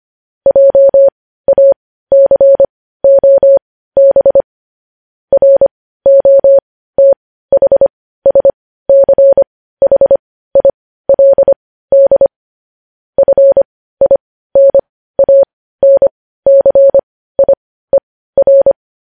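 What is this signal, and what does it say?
Morse code, heard as a keyed tone, about 25 wpm, spaced as for 15 wpm, 550 Hz. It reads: JACOB ROTHSCHILD FINANCIER
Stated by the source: Morse code news headlines sent at 15, preceded by vvv.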